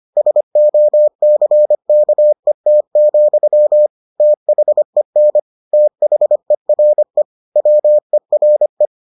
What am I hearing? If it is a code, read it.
Morse code, heard as a keyed tone, about 25 wpm, 600 Hz. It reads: SOCKET, THEN THERE WERE